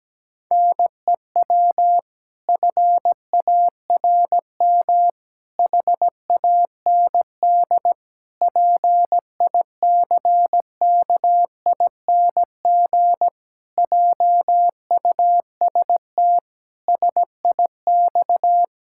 Morse code, heard as a keyed tone, about 17 wpm, 700 Hz. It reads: NEW FARM HAND PICKING JUST SIX
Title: Pangram Challenge v2 - 17wpm